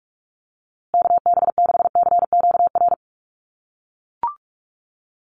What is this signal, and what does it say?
Morse code, heard as a keyed tone, about 44 wpm, 700 Hz. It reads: KB6CQR
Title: USA Call Signs Repeated 3x 45wpm